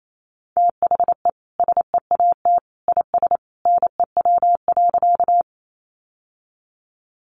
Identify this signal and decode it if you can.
Morse code, heard as a keyed tone, about 28 wpm, 700 Hz. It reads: THE SEAT IS NEW.
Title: Sentences from Top 1500 Words 28wpm